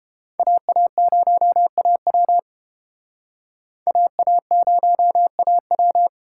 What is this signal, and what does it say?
Morse code, heard as a keyed tone, about 28 wpm, 700 Hz. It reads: AA0AW AA0AW